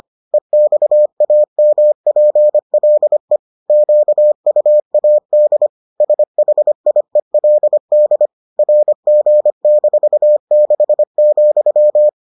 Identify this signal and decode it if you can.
Morse code, heard as a keyed tone, about 25 wpm, 600 Hz. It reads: EXAMPLE QUAD SHIELD RG-6,